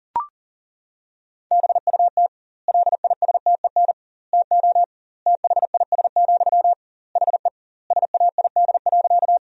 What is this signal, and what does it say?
Morse code, heard as a keyed tone, about 37 wpm, 700 Hz. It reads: BUT LISTEN TO THIS, HE SAID.
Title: 2x Word-Spacing - Sentences from Top 400 Words - 40wpm